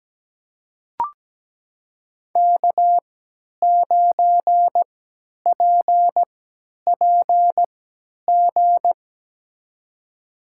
Morse code, heard as a keyed tone, about 17 wpm, spaced as for 10 wpm, 700 Hz. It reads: K9PPG
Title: USA Call Signs - 3x Character Spacing - 17wpm